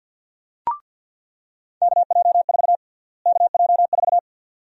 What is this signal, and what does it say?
Morse code, heard as a keyed tone, about 50 wpm, 700 Hz. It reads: KJ4 KJ4
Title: USA Call Sign Prefixes - Repeated 2x - 50wpm